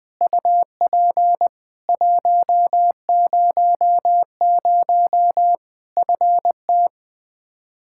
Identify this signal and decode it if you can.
Morse code, heard as a keyed tone, about 20 wpm, 700 Hz. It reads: UP 100 FT
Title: Practice QSO - Exchange 2 - Info - 20wpm